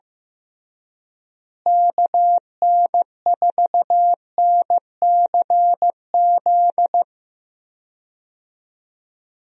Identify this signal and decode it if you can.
Morse code, heard as a keyed tone, about 16 wpm, 700 Hz. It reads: KN4NCZ
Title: USA Call Signs - 6 Characters - Repeated 3x - 15wpm